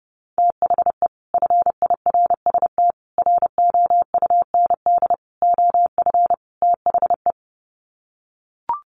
Morse code, heard as a keyed tone, about 30 wpm, 700 Hz. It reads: THE FIRST ROUND OF THE
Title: Sets of 5 Words Encore 30wpm